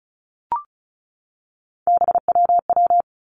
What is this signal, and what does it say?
Morse code, heard as a keyed tone, about 36 wpm, 700 Hz. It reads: BWW
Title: Three Challenging Letter-Number Combinations 35wpm